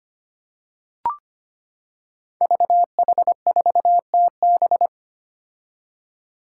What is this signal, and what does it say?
Morse code, heard as a keyed tone, about 25 wpm, 700 Hz. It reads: VH4TB
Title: Five Challenging Letter-Number Combinations 25wpm